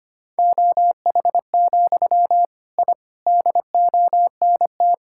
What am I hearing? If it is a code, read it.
Morse code, heard as a keyed tone, about 25 wpm, 700 Hz. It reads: OH, I DONT
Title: Sentences from Top 1000 Words - Review Part 6 - 25wpm